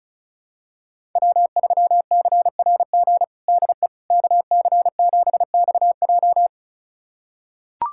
Morse code, heard as a keyed tone, about 34 wpm, 700 Hz. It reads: W3CRG DE KC7XJ
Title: USA Callsign DE Callsign - 5 Characters - 35wpm